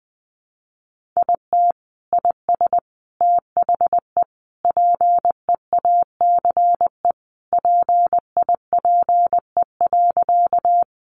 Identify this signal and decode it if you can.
Morse code, heard as a keyed tone, about 19 wpm, 700 Hz. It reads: IT IS THE PEACE PIPE.
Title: Sentences from Top 1500 Words 20wpm